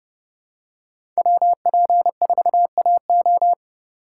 Morse code, heard as a keyed tone, about 30 wpm, 700 Hz. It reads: WP4AO
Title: USA Call Signs - 5 Characters - 2x Character Spacing - 30wpm